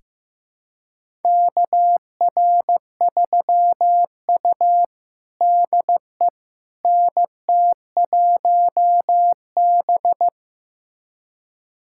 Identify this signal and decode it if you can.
Morse code, heard as a keyed tone, about 16 wpm, 700 Hz. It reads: KR3U DE NT1B
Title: USA Callsign DE Callsign - 4 Characters - 15wpm